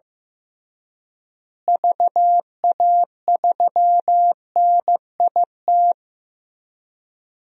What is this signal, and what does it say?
Morse code, heard as a keyed tone, about 15 wpm, 700 Hz. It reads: VA3NIT
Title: DXCC-100 Call Signs - 6 Characters - Repeated 2x - 15wpm